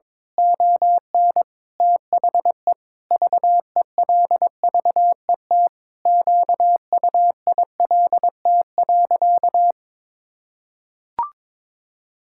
Morse code, heard as a keyed tone, about 22 wpm, 700 Hz. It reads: ON THE VELVET QUILT.
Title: Pangram Challenge v2 - 22wpm